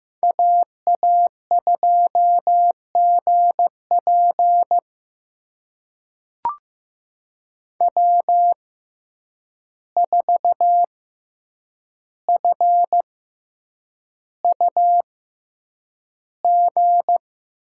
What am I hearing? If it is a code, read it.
Morse code, heard as a keyed tone, about 15 wpm, spaced as for 8 wpm, 700 Hz. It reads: AA2GP W4FUG
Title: USA Call Signs - 5 Characters - 6x Character Spacing - 15wpm